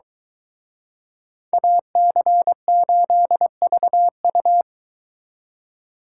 Morse code, heard as a keyed tone, about 23 wpm, 700 Hz. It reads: AC8VU